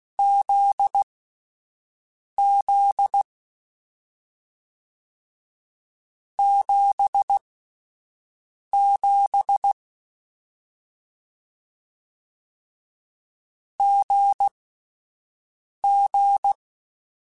Morse code, heard as a keyed tone, about 16 wpm, spaced as for 5 wpm, 800 Hz. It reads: ZZ 77 GG
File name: Difficult Combos / Difficult Combos-mp3s / GZ GZ7 @5